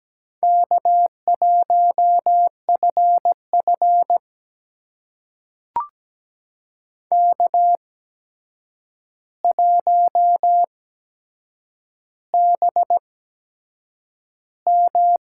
Morse code, heard as a keyed tone, about 17 wpm, spaced as for 7 wpm, 700 Hz. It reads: K1FF K1BM
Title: USA Call Signs - 4 Characters - 8x Character Spacing - 17wpm